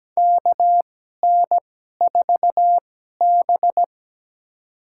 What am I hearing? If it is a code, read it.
Morse code, heard as a keyed tone, about 17 wpm, spaced as for 12 wpm, 700 Hz. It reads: KN4B
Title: USA Call Signs - 4 Characters - 2x Character Spacing - 17wpm